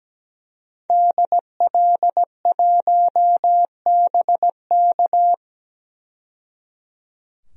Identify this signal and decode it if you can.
Morse code, heard as a keyed tone, about 17 wpm, 700 Hz. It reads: DL1BK
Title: DXCC-100 Call Signs - 5 Characters - 17wpm